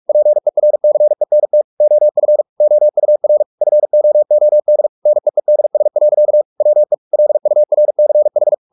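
Morse code, heard as a keyed tone, about 44 wpm, 600 Hz. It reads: PERCENT OF OUR FOOD NEEDS. WE LURCH
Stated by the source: Morse practice at 45 wpm, the novel Never by Ken Follett in English, from CW med Gnister.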